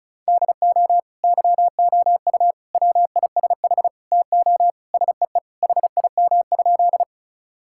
Morse code, heard as a keyed tone, about 36 wpm, 700 Hz. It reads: DO YOU WISH TO SEE HIM?
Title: Sentences from Top 1600 Words 35wpm